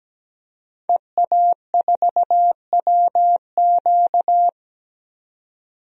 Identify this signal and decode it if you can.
Morse code, heard as a keyed tone, about 17 wpm, 700 Hz. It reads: EA4WQ
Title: DXCC-100 Call Signs - 5 Characters - 17wpm